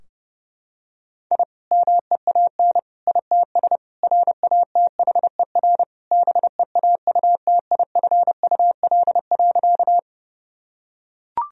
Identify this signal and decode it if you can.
Morse code, heard as a keyed tone, about 30 wpm, 700 Hz. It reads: I MEAN ITS RATHER BEAUTIFUL.
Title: Sentences from Top 1000 Words - Review Part 4 - 30wpm